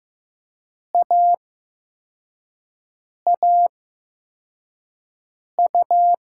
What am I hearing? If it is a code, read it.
Morse code, heard as a keyed tone, about 15 wpm, spaced as for 4 wpm, 700 Hz. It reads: AAU
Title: USA Call Signs - 6 Characters - 8x Character Spacing - 15wpm